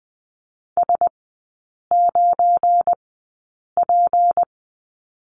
Morse code, heard as a keyed tone, about 20 wpm, spaced as for 8 wpm, 700 Hz. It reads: S9P
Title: 198 Lesson 32e - Letter k - Three Characters 20wpm